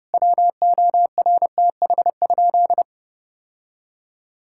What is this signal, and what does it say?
Morse code, heard as a keyed tone, about 28 wpm, 700 Hz. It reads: WORTH?
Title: Sentences from Top 1000 Words - Review Part 6 - 30wpm